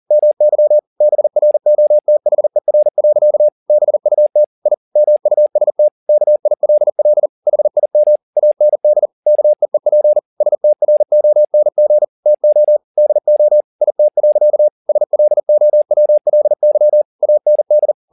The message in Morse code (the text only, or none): MY BROTHER. BUT I MUST KILL HIM AND KEEP STRONG TO DO IT. SLOWLY AND